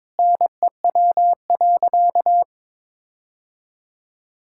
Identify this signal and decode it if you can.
Morse code, heard as a keyed tone, about 22 wpm, 700 Hz. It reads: NEW.
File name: Sentences with 9-Word Phrases - 22wpm